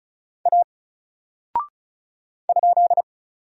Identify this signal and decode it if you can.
Morse code, heard as a keyed tone, about 34 wpm, 700 Hz. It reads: A ?
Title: Single Character (Letter:Number:Punctuation:Prosign) 35wpm